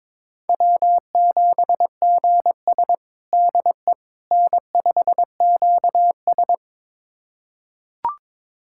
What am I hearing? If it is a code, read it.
Morse code, heard as a keyed tone, about 21 wpm, 700 Hz. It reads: W7GS DE N5QS
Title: USA Callsign DE Callsign - 4 Characters - 22wpm